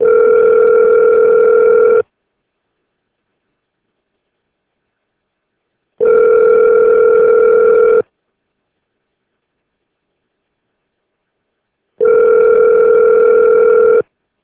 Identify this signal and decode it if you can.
Ringback tone (North America), 440 + 480 Hz.